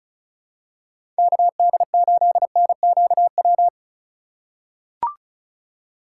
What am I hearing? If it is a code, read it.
Morse code, heard as a keyed tone, about 34 wpm, 690 Hz. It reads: KD8NQW
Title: USA Call Signs - 6 Characters - 2x Character Spacing - 35wpm